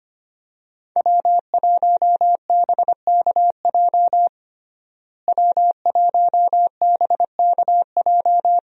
Morse code, heard as a keyed tone, about 25 wpm, 700 Hz. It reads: W1BKJ W1BKJ
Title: USA Call Signs - 5 Characters - Repeated 2x - 25wpm